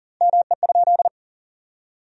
Morse code, heard as a keyed tone, about 40 wpm, 700 Hz. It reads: ME?
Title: 2x Word-Spacing - Sentences from Top 300 Words - 40wpm